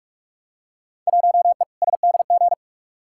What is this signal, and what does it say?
Morse code, heard as a keyed tone, about 46 wpm, 700 Hz. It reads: 1E SDG
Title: Field Day Exchanges - 45wpm